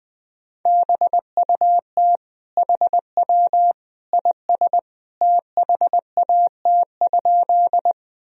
Morse code, heard as a keyed tone, about 20 wpm, 700 Hz. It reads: BUT HW IS THAT?
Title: Sentences from Top 100 Words and Abbreviations 20wpm